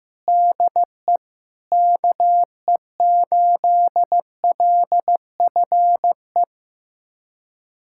Morse code, heard as a keyed tone, about 16 wpm, 700 Hz. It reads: DE KE8LFE
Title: USA Callsign DE Callsign - 6 Characters - 15wpm